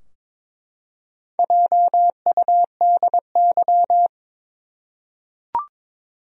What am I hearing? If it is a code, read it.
Morse code, heard as a keyed tone, about 22 wpm, 700 Hz. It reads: JUDY
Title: USA Common Names - Max 4 Letters - 22wpm